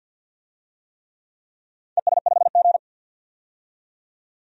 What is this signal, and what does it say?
Morse code, heard as a keyed tone, about 52 wpm, 700 Hz. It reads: ES5G
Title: DXCC-100 Call Signs - 4 Characters - 50wpm